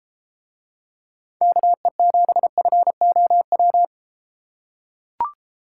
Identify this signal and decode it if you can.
Morse code, heard as a keyed tone, about 33 wpm, 700 Hz. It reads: KE7FOW